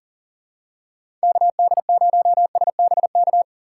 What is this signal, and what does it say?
Morse code, heard as a keyed tone, about 40 wpm, 700 Hz. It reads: KD0SBK
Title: USA Call Signs - 6 Characters - 6x Character Spacing - 40wpm